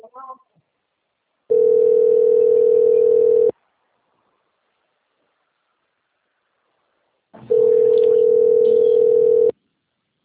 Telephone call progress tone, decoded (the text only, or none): ringback tone